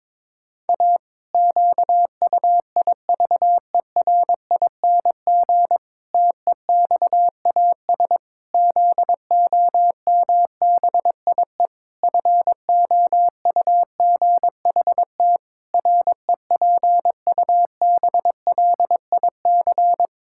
Morse code, heard as a keyed tone, about 22 wpm, 700 Hz. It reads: A QUIVERING TEXAS ZOMBIE FOUGHT REPUBLIC
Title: Pangram Challenge v2 - 22wpm